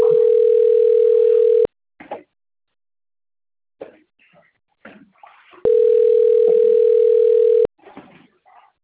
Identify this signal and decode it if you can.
Ringback tone (North America), 440 + 480 Hz.
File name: rg-1600-unknown- - - .28538